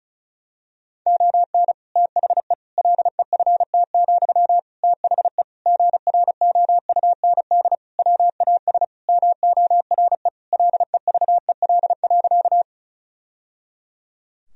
Morse code, heard as a keyed tone, about 34 wpm, 700 Hz. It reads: ON THE LEFT, THE GROUND WAS MORE LEVEL.
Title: Sentences from Top 1000 Words - Review Part 4 - 35wpm